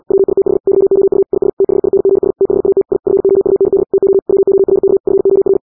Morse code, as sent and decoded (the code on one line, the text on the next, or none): .-- .. - .... / -.-- --- ..- / .. ... / - .... .. -. --. ... / - .... .- - / .. / ..-. --- ..- -. -.. / - --- / .-- --- .-. -.- / ..-. --- .-.
WITH YOU IS THINGS THAT I FOUND TO WORK FOR